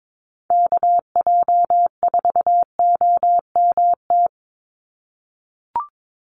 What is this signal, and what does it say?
Morse code, heard as a keyed tone, about 22 wpm, 700 Hz. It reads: KJ4OMT
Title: USA Call Signs 22wpm